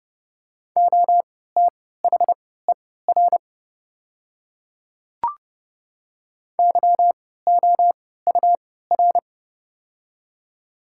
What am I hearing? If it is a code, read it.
Morse code, heard as a keyed tone, about 30 wpm, spaced as for 17 wpm, 700 Hz. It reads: OTHER YOUR
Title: 3x Character Spacing - Top 100 Words - 30wpm